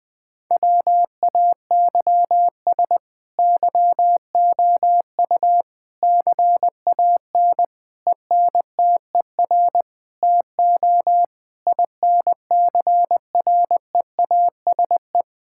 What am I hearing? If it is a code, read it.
Morse code, heard as a keyed tone, about 20 wpm, 700 Hz. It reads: WAYS YOU CAN ENTER TO INCREASE
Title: Sets of 8 Words - 20wpm